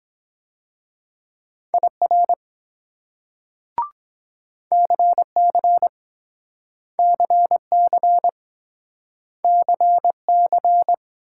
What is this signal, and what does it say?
Morse code, heard as a keyed tone, about 23 wpm, 700 Hz. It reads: IR CC CC CC